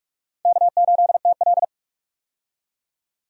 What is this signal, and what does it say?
Morse code, heard as a keyed tone, about 46 wpm, 700 Hz. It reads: K8TL
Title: USA Call Signs - 4 Characters - 45wpm